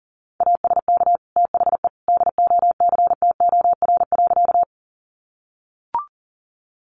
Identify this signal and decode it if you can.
Morse code, heard as a keyed tone, about 40 wpm, 700 Hz. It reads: ASK THE DOCTOR.
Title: Sentences from Top 1500 Words 40wpm